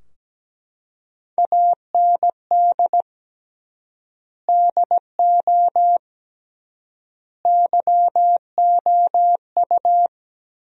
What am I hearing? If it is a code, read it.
Morse code, heard as a keyed tone, about 17 wpm, 700 Hz. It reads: AND DO YOU
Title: 3x Word-Spacing - Sentences from Top 200 Words - 17wpm